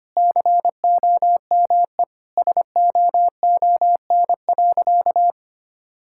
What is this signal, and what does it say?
Morse code, heard as a keyed tone, about 25 wpm, 700 Hz. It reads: COME SOON.
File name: practice / 25wpm / Sentences with 2-Word Phrases - 25wpm